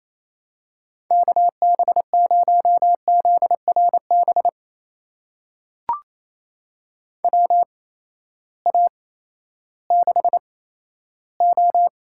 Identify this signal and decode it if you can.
Morse code, heard as a keyed tone, about 28 wpm, spaced as for 14 wpm, 700 Hz. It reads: KB0ZRB WA6O
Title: USA Call Signs - 6 Characters - 8x Character Spacing - 28wpm